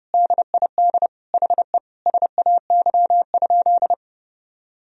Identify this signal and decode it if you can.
Morse code, heard as a keyed tone, about 30 wpm, 700 Hz. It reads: DID HE SAY?
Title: Sentences from Top 200 Words and Abbreviations 30wpm